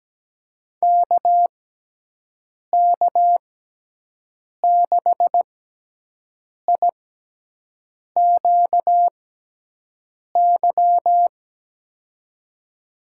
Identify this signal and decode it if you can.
Morse code, heard as a keyed tone, about 17 wpm, spaced as for 6 wpm, 700 Hz. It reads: KK6IQY
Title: USA Call Signs - 6 Characters - 6x Character Spacing - 17wpm